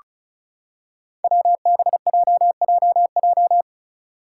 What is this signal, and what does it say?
Morse code, heard as a keyed tone, about 36 wpm, 700 Hz. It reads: WBJJJ
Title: Five Challenging Letter-Number Combinations 35wpm